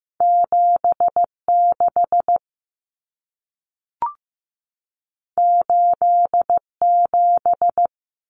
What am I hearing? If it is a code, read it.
Morse code, heard as a keyed tone, about 15 wpm, 700 Hz. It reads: 76 87
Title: Common Cut Numbers - 2-Digit 15wpm